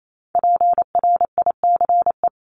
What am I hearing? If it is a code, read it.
Morse code, heard as a keyed tone, about 28 wpm, 700 Hz. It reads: PRICE
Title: Top 501-750 Words 28wpm